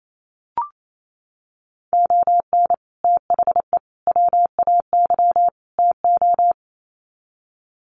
Morse code, heard as a keyed tone, about 28 wpm, 700 Hz. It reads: ON THE WAY TO
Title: Sets of 4 Words 28wpm